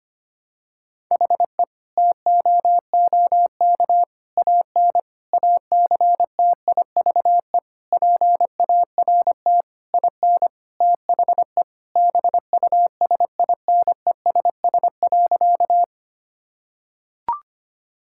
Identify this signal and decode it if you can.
Morse code, heard as a keyed tone, about 25 wpm, 700 Hz. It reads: HE TOOK AN ACTIVE PART IN THE BUSINESS.
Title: Sentences from Top 1000 Words - Review Part 6 - 25wpm